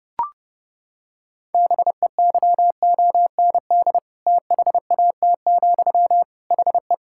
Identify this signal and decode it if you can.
Morse code, heard as a keyed tone, about 30 wpm, 700 Hz. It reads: BEYOND THAT, HE
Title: Sentences from Top 1000 Words - Review Part 6 - 30wpm